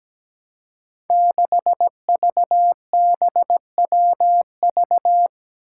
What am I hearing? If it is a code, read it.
Morse code, heard as a keyed tone, about 17 wpm, 700 Hz. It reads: 6VBWV
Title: Five Challenging Letter-Number Combinations 17wpm